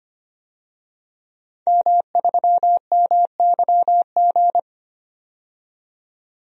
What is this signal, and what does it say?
Morse code, heard as a keyed tone, about 25 wpm, 700 Hz. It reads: M3MYG
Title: DXCC-100 Call Signs - 5 Characters - 25wpm